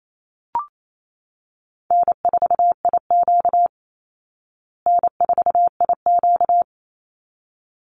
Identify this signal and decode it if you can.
Morse code, heard as a keyed tone, about 28 wpm, 700 Hz. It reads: N4IQ N4IQ
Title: Call Signs Repeated 3x - 28wpm